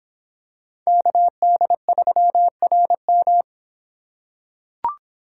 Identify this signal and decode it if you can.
Morse code, heard as a keyed tone, about 26 wpm, 700 Hz. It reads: KD3RM